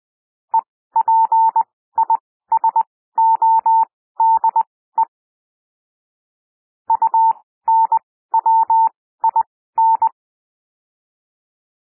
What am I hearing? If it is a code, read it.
Morse code, heard as a keyed tone, about 20 wpm, spaced as for 14 wpm, 900 Hz. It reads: EPISODE UNWIN
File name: SOUND WORDS 20 wpm